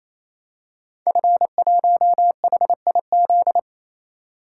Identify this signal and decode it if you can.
Morse code, heard as a keyed tone, about 27 wpm, 700 Hz. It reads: F1HIZ
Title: Code Groups - 5 Letter:Numbers - 28wpm